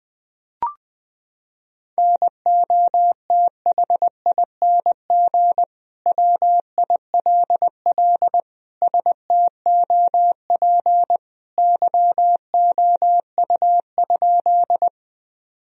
Morse code, heard as a keyed tone, about 20 wpm, 700 Hz. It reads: NOTHING WILL STOP YOU?
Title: Sentences from Top 1000 Words - Review Part 1 - 20wpm